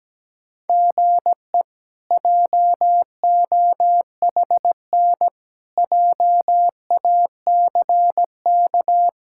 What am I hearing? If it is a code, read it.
Morse code, heard as a keyed tone, about 17 wpm, 700 Hz. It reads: GE JOHN JACK
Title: K1USN SST QSO Practice - 17wpm